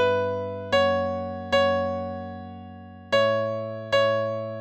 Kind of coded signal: none